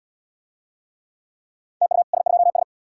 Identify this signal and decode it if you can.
Morse code, heard as a keyed tone, about 69 wpm, 700 Hz. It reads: TU S2S